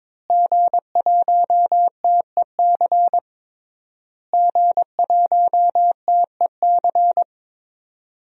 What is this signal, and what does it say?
Morse code, heard as a keyed tone, about 22 wpm, 700 Hz. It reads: G1TEC G1TEC
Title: DXCC-100 Call Signs - 5 Characters - Repeated 3x - 22wpm